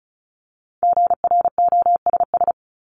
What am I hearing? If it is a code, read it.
Morse code, heard as a keyed tone, about 34 wpm, 700 Hz. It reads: GROSS
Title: Top 1751-2000 Words 35wpm